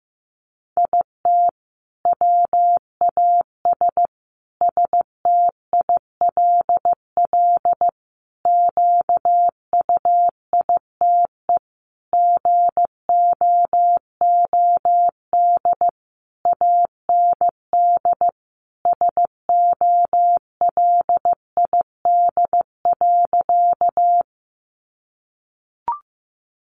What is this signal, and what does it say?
Morse code, heard as a keyed tone, about 15 wpm, 700 Hz. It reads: IT WAS STILL QUITE GOOD AND SOLID.
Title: Sentences from Top 1500 Words 15wpm